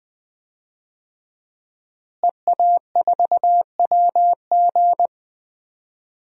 Morse code, heard as a keyed tone, about 21 wpm, 700 Hz. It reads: EA4WG
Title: DXCC-100 Call Signs - 5 Characters - Repeated 3x - 20wpm